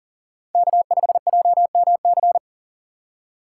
Morse code, heard as a keyed tone, about 40 wpm, 700 Hz. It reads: K5JMC
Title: USA Call Signs - 5 Characters - 2x Character Spacing - 40wpm